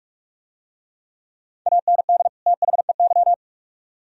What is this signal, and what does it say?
Morse code, heard as a keyed tone, about 44 wpm, 700 Hz. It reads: AND THEY